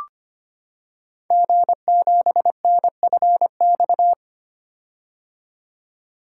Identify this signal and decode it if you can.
Morse code, heard as a keyed tone, about 25 wpm, 700 Hz. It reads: G7NFX
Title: Code Groups - 5 Letter:Numbers - 25wpm